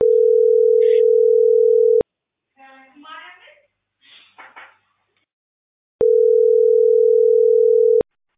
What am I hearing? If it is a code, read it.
Ringback tone (North America), 440 + 480 Hz.